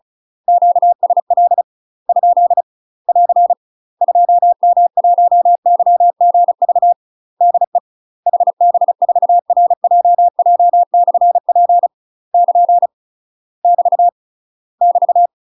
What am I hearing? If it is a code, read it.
Morse code, heard as a keyed tone, about 34 wpm, 700 Hz. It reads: QSL ? <AR> 2M1YGV DE HB4RJJ/P <KN> <BT> <BT>
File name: UK MASTER QSOs TRAINING / 35WPM / UK MASTER QSOs 0043